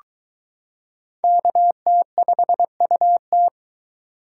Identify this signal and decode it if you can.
Morse code, heard as a keyed tone, about 22 wpm, 700 Hz. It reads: KT5UT